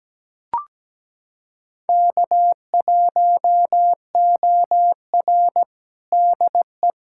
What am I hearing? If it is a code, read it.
Morse code, heard as a keyed tone, about 17 wpm, 700 Hz. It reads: K1OR DE
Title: USA Callsign DE Callsign - 4 Characters - 17wpm